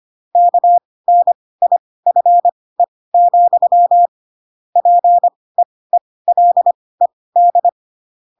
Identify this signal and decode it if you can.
Morse code, heard as a keyed tone, about 25 wpm, spaced as for 18 wpm, 700 Hz. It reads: KNIFE, PEELED
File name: Pinocchio / Pinocchio ch7 12 25 ik1hge